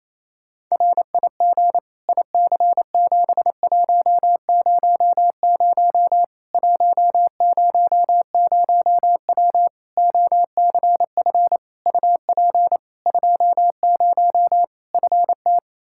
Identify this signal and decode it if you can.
Morse code, heard as a keyed tone, about 28 wpm, 700 Hz. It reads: RIG IC7100 100W OCF UP 20 FT
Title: Practice QSO - Exchange 2 - Info - 28wpm